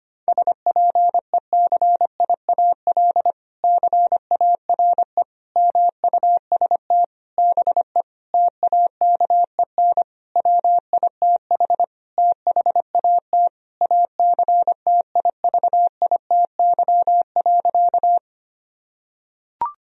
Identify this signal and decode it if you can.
Morse code, heard as a keyed tone, about 25 wpm, 700 Hz. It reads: SPECIAL CARE MUST BE TAKEN WITH THAT ACTIVITY.
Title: Sentences from Top 1000 Words - Review Part 4 - 25wpm